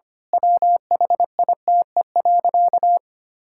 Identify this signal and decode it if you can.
Morse code, heard as a keyed tone, about 25 wpm, 700 Hz. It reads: WHITE.
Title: Pangram Challenge v2 - 25wpm